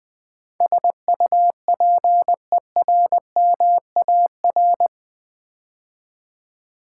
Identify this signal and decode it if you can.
Morse code, heard as a keyed tone, about 20 wpm, 700 Hz. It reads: SUPERMAR
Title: Build A Word - 4 Letters At A Time - 11-Letter Words - 20wpm